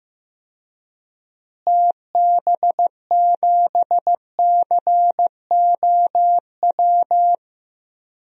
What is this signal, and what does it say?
Morse code, heard as a keyed tone, about 16 wpm, 700 Hz. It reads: TB7COW
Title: DXCC-100 Call Signs - 6 Characters - Repeated 2x - 15wpm